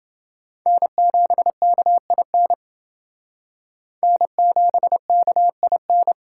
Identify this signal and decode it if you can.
Morse code, heard as a keyed tone, about 28 wpm, 700 Hz. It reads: N7KIN N7KIN